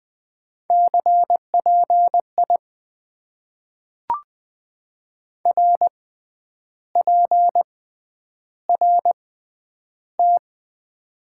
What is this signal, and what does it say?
Morse code, heard as a keyed tone, about 20 wpm, spaced as for 9 wpm, 700 Hz. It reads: CPI RPRT
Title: Top 90 QSO Elements - 6x Character Spacing - 20wpm